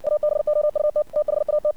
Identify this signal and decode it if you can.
Morse code, heard as a keyed tone, about 59 wpm, 600 Hz. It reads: ABOUT THAT
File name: archived / CRSnet